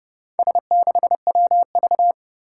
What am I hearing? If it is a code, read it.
Morse code, heard as a keyed tone, about 28 wpm, 690 Hz. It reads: S6WV
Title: Four Challenging Letter-Number Combinations 30wpm